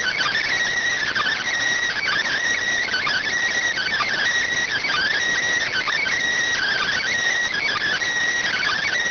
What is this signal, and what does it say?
Emergency Alert System (SAME) tones (header bursts), heard as harsh buzzing bursts. It is heard from 1.1 to 8.5 s.